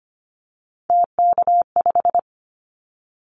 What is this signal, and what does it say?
Morse code, heard as a keyed tone, about 25 wpm, 700 Hz. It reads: TK5
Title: Callsign Prefixes for Top 100 DXCC - 25wpm